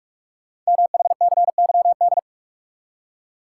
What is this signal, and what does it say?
Morse code, heard as a keyed tone, about 44 wpm, 700 Hz. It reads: MHCYD
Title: Code Groups - 5 Letters - 45wpm